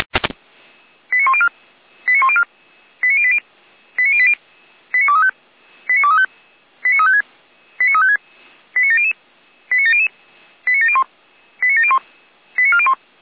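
Five-tone selective call (ZVEI), heard as quick runs of five tones, tones of about 70 ms. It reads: 89104 89104 89080 89080 89235 89235 89356 89700 89700 89701 89701 89401
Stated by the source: ZVEI five-tone selective call received by SDR.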